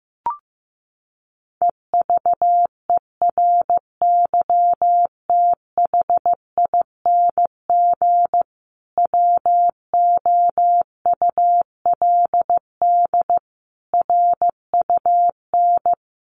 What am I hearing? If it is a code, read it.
Morse code, heard as a keyed tone, about 15 wpm, 700 Hz. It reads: EVERYTHING WOULD RUN
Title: Sentences from Top 1400 Words 15wpm